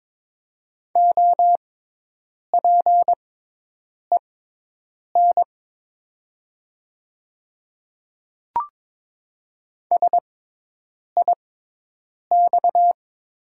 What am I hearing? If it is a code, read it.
Morse code, heard as a keyed tone, about 22 wpm, spaced as for 8 wpm, 700 Hz. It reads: OPEN SIX
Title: 6x Character Spacing - Top 500 Words - 22wpm